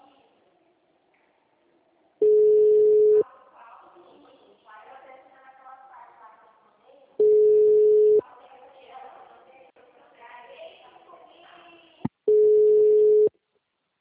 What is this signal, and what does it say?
Ringback tone (Europe, 425 Hz), with speech.
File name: out- -2014- - - .26283